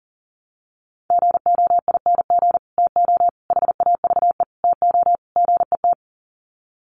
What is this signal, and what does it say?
Morse code, heard as a keyed tone, about 40 wpm, 700 Hz. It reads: GOING TO HAVE TO GET
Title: Sets of 5 Words Encore2 40wpm